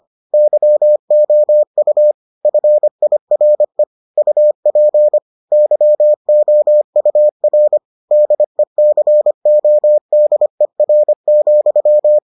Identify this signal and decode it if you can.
Morse code, heard as a keyed tone, about 25 wpm, 600 Hz. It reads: YOU FIRE UP YOUR DECODER,